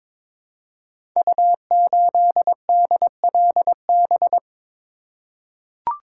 Morse code, heard as a keyed tone, about 23 wpm, 700 Hz. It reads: U8DLB
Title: Code Groups - 5 Letter:Numbers - 22wpm